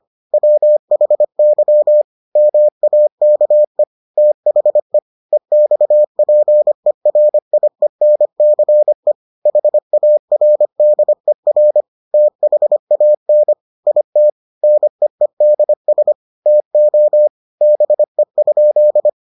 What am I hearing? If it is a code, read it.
Morse code, heard as a keyed tone, about 25 wpm, 600 Hz. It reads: WHY MAKE THE EXPERIENCE HARDER THAN IT NEEDS TO BE?